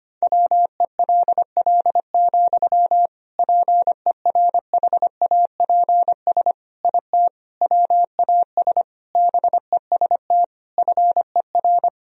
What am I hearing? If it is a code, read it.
Morse code, heard as a keyed tone, about 25 wpm, 700 Hz. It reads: WELL, PERHAPS IT WAS BEST FER
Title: Sentences from Top 500 Words and Abbreviations 25wpm